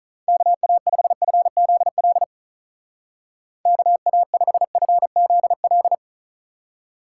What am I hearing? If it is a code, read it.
Morse code, heard as a keyed tone, about 39 wpm, 700 Hz. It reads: KA5FZL KA5FZL
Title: Speed Racing - USA 2x3 Call Signs 35wpm